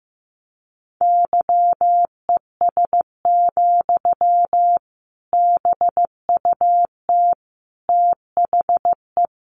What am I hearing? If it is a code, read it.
Morse code, heard as a keyed tone, about 15 wpm, 700 Hz. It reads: YES, BUT THE
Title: Sentences from Top 1200 Words 15wpm